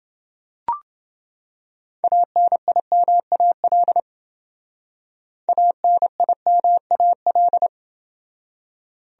Morse code, heard as a keyed tone, about 28 wpm, 700 Hz. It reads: ANIMAL ANIMAL